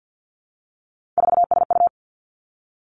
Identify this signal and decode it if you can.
Morse code, heard as a keyed tone, about 50 wpm, 700 Hz. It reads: 4SU
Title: Three Challenging Letter-Number Combinations 50wpm